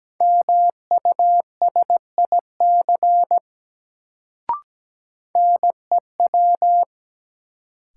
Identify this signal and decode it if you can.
Morse code, heard as a keyed tone, about 17 wpm, 700 Hz. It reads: MUSIC NEW
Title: Sentences from Top 300 Words - New Word Drill - 17wpm